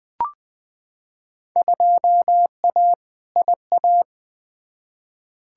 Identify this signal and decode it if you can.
Morse code, heard as a keyed tone, about 21 wpm, 700 Hz. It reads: 2A IA
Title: Field Day Exchanges Encore - 20wpm